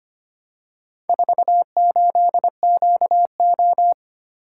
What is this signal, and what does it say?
Morse code, heard as a keyed tone, about 25 wpm, 700 Hz. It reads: 48QO
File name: practice / 25wpm / Four Letter-Number Combinations - 25wpm